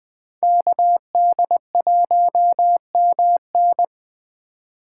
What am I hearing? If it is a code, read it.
Morse code, heard as a keyed tone, about 19 wpm, 700 Hz. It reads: KD1MN